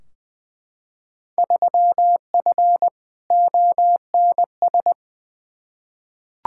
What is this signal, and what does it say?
Morse code, heard as a keyed tone, about 20 wpm, 700 Hz. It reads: 3F ONS
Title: Field Day Exchanges - 20wpm